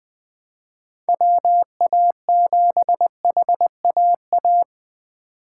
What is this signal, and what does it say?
Morse code, heard as a keyed tone, about 21 wpm, 700 Hz. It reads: WA7HAA
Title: Speed Racing - USA 2x3 Call Signs 17wpm